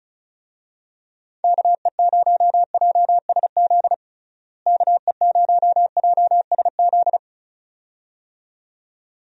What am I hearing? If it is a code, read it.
Morse code, heard as a keyed tone, about 36 wpm, 700 Hz. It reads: KE0JSZ KE0JSZ
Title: USA Call Signs - 6 Characters - Repeated 2x - 35wpm